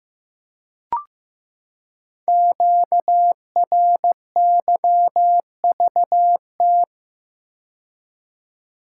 Morse code, heard as a keyed tone, about 15 wpm, 700 Hz. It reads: QRYVT
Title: Code Groups - 5 Letter:Numbers - 15wpm